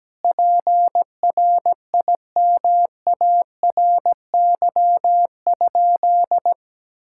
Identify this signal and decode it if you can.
Morse code, heard as a keyed tone, about 17 wpm, 700 Hz. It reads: PRIMARY?
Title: Sentences from Top 1000 Words - Review Part 5 - 17wpm